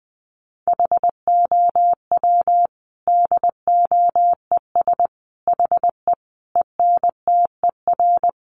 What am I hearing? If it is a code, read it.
Morse code, heard as a keyed tone, about 20 wpm, 700 Hz. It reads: HOW DOES HE ENTER
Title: Sentences from Top 1300 Words 20wpm